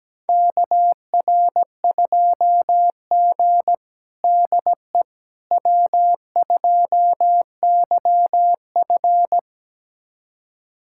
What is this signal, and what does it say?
Morse code, heard as a keyed tone, about 17 wpm, 700 Hz. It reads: KR2G DE W2YF
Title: USA Callsign DE Callsign - 4 Characters - 17wpm